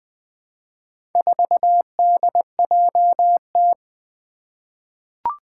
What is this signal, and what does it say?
Morse code, heard as a keyed tone, about 20 wpm, 700 Hz. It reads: 4DJT
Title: Four Challenging Letter-Number Combinations 20wpm